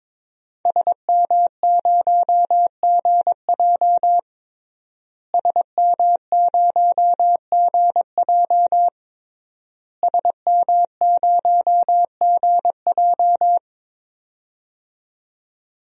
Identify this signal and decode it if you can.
Morse code, heard as a keyed tone, about 22 wpm, 700 Hz. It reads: SM0GJ SM0GJ SM0GJ
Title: DXCC-100 Call Signs - 5 Characters - Repeated 3x - 22wpm